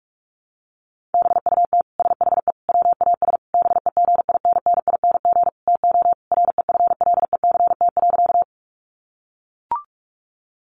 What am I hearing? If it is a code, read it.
Morse code, heard as a keyed tone, about 44 wpm, 700 Hz. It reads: BUT SHE WAS BEGINNING TO REFLECT.
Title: Sentences from Top 1500 Words 45wpm